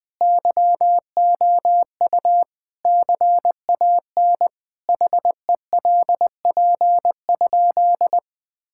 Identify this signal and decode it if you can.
Morse code, heard as a keyed tone, about 20 wpm, 700 Hz. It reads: YOU CAN HELP?
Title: Sentences with 4-Word Phrases - 20wpm